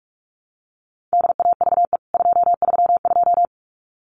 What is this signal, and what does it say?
Morse code, heard as a keyed tone, about 44 wpm, 700 Hz. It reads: DAVE 232
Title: CWT Exchange 45wpm